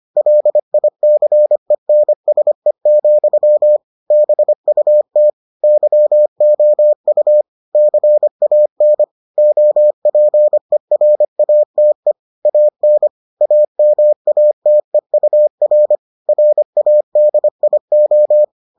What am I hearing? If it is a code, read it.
Morse code, heard as a keyed tone, about 25 wpm, 600 Hz. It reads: LICENSE, BUT YOU CAN OPERATE AN AMATEUR RADIO